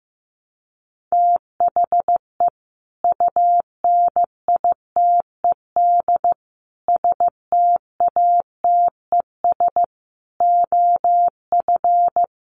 Morse code, heard as a keyed tone, about 15 wpm, 700 Hz. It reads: THE UNITED STATES OF
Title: Sets of 5 Words 15wpm